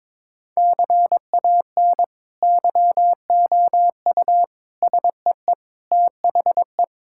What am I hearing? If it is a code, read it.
Morse code, heard as a keyed tone, about 22 wpm, 700 Hz. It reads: CAN YOU SEE THE
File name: practice / 22wpm / Sentences with 2-Word Phrases - 22wpm